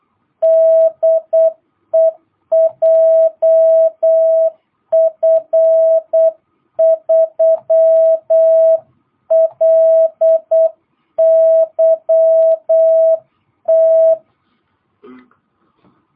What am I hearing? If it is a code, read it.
Morse code, heard as a keyed tone, about 8 wpm, 660 Hz. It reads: DE JF3LYT
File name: jh8cbh / 2022cw